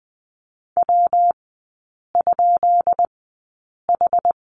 Morse code, heard as a keyed tone, about 20 wpm, spaced as for 8 wpm, 700 Hz. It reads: W?H
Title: 133 Lesson 22e - Question Mark - Three Characters 20wpm